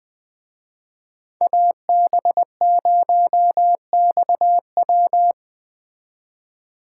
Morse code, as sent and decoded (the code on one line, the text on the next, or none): .- -... ----- -..- .--
AB0XW